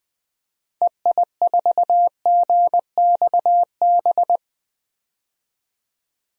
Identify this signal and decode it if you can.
Morse code, heard as a keyed tone, about 21 wpm, 700 Hz. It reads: EI4GXB
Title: DXCC-100 Call Signs - 6 Characters - 20wpm